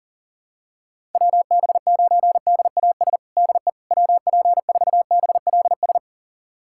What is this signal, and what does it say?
Morse code, heard as a keyed tone, about 40 wpm, 700 Hz. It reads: WB9DAS DE WP4BLS
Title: USA Callsign DE Callsign - 6 Characters - 40wpm